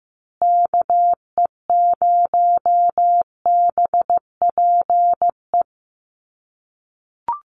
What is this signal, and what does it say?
Morse code, heard as a keyed tone, about 15 wpm, 700 Hz. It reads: KE0BPE
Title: USA Call Signs Repeated 3x 15wpm